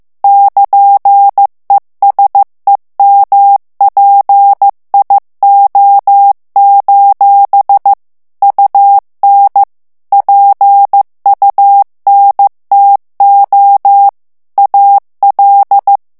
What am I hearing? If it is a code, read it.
Morse code, heard as a keyed tone, about 15 wpm, 800 Hz. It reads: <KN>ESEMPIO: UN PUNTO AL
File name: morse mp3 / ITALIEN texte clair 2 vitesse 14 mots minute